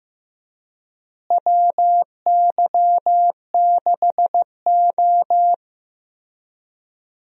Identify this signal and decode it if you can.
Morse code, heard as a keyed tone, about 16 wpm, 700 Hz. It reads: WY6O
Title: USA Call Signs - 4 Characters - Repeated 2x - 15wpm